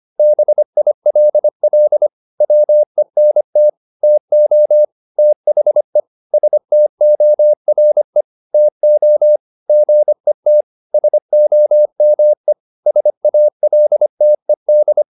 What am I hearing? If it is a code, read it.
Morse code, heard as a keyed tone, about 25 wpm, 600 Hz. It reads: BILL WENT TO THE STORE TO GET SOME SALTED